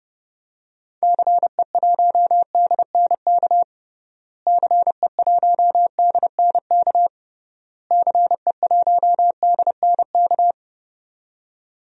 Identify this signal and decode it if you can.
Morse code, heard as a keyed tone, about 30 wpm, 700 Hz. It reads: CE1DNK CE1DNK CE1DNK
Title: DXCC-100 Call Signs - 6 Characters - Repeated 3x - 30wpm